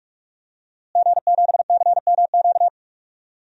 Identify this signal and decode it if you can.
Morse code, heard as a keyed tone, about 46 wpm, 700 Hz. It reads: G7CMQ
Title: DXCC-100 Call Signs - 5 Characters - Repeated 2x - 45wpm